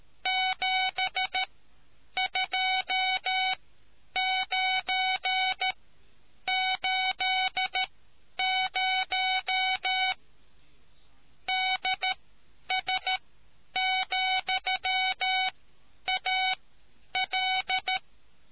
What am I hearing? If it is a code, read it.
Morse code, heard as a keyed tone, about 13 wpm, spaced as for 9 wpm, 750 Hz. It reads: 72980 DS,AL